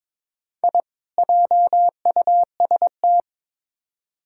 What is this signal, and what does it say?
Morse code, heard as a keyed tone, about 22 wpm, 700 Hz. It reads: I JUST
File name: Build A Phrase - 2-Word Phrases - 22wpm